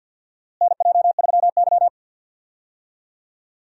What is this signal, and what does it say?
Morse code, heard as a keyed tone, about 52 wpm, 700 Hz. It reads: NJ3Y
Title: USA Call Signs - 4 Characters - Repeated 3x - 50wpm